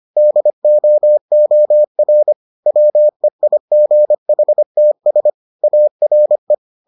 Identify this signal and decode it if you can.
Morse code, heard as a keyed tone, about 25 wpm, 600 Hz. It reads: DOOR WEIGHTS ARE